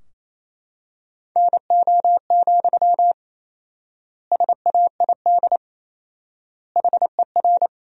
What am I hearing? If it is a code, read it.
Morse code, heard as a keyed tone, about 27 wpm, 700 Hz. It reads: NO, SAID HER
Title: 4x Word-Spacing - Sentences from Top 300 Words - 28wpm